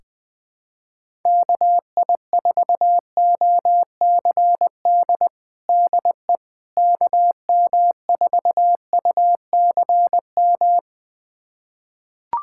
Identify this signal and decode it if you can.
Morse code, heard as a keyed tone, about 20 wpm, 700 Hz. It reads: KI4OCD DE KM4UCM
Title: USA Callsign DE Callsign - 6 Characters - 20wpm